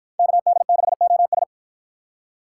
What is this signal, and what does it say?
Morse code, heard as a keyed tone, about 53 wpm, 700 Hz. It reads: KD6OS